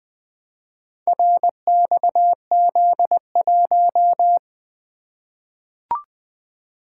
Morse code, heard as a keyed tone, about 20 wpm, 700 Hz. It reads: RXZ1
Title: Four Letter-Number Combinations - 20wpm